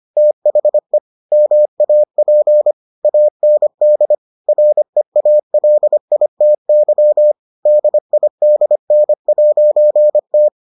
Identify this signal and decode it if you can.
Morse code, heard as a keyed tone, about 25 wpm, 600 Hz. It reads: THE MAP AND REALITY DIDN'T